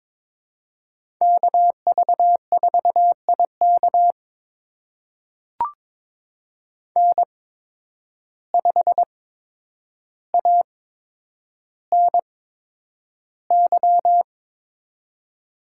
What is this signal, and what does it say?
Morse code, heard as a keyed tone, about 22 wpm, spaced as for 9 wpm, 700 Hz. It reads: KV4IK N5ANY
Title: USA Call Signs - 5 Characters - 8x Character Spacing - 22wpm